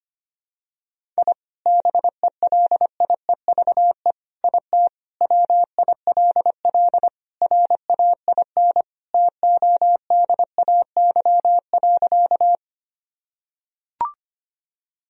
Morse code, heard as a keyed tone, about 25 wpm, 700 Hz. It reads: I BELIEVE IT WILL RAIN TODAY.
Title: Sentences from Top 1000 Words - Review Part 1 - 25wpm